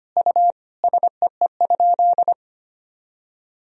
Morse code, heard as a keyed tone, about 25 wpm, 700 Hz. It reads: U SEE?
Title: Sentences from Top 100 Words and Abbreviations 25wpm